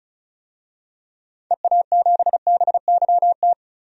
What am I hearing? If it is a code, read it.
Morse code, heard as a keyed tone, about 36 wpm, 700 Hz. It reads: EA7BYT
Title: DXCC-100 Call Signs - 6 Characters - 35wpm